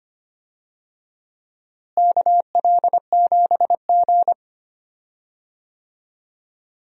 Morse code, heard as a keyed tone, about 25 wpm, 700 Hz. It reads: KL7G
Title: DXCC-100 Call Signs - 4 Characters - Repeated 3x - 25wpm